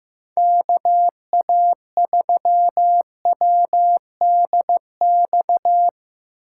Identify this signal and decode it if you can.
Morse code, heard as a keyed tone, about 17 wpm, 700 Hz. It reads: KA3WDX
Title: Speed Racing - USA 2x3 Call Signs 15wpm